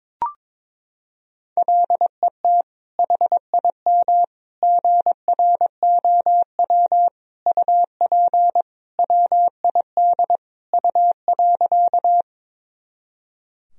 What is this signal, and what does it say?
Morse code, heard as a keyed tone, about 22 wpm, 700 Hz. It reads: LET HIM GROW UP WID U.
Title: Sentences from Top 300 Words and Abbreviations 22wpm